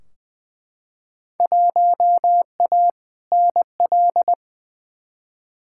Morse code, heard as a keyed tone, about 20 wpm, 700 Hz. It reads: 1A NL
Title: Field Day Exchanges Encore - 20wpm